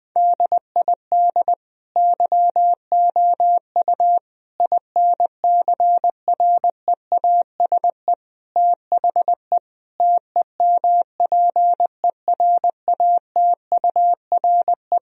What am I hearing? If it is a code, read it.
Morse code, heard as a keyed tone, about 20 wpm, 700 Hz. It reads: DID YOU INCREASE THE TEMPERATURE
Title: Sentences from Top 1000 Words - Review Part 4 - 20wpm